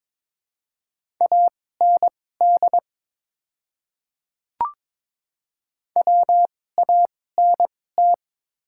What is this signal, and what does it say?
Morse code, heard as a keyed tone, about 23 wpm, spaced as for 16 wpm, 700 Hz. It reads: AND WANT
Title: 2x Character Spacing - Top 200 Words - 22wpm